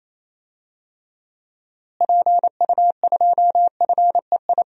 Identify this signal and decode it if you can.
Morse code, heard as a keyed tone, about 28 wpm, 700 Hz. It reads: PU2FEI
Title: DXCC-100 Call Signs - 6 Characters - Repeated 3x - 28wpm